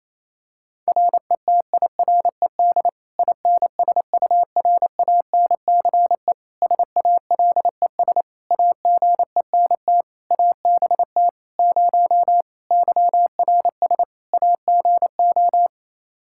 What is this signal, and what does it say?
Morse code, heard as a keyed tone, about 28 wpm, 700 Hz. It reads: RETIRED INSURANCE SALES AGENT ABT 0 YRS AGO
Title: Practice QSO - Exchange 3 - More Info - 28wpm